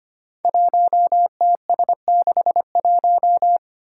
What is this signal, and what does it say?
Morse code, heard as a keyed tone, about 25 wpm, 700 Hz. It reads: 1TS61
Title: Five Challenging Letter-Number Combinations 25wpm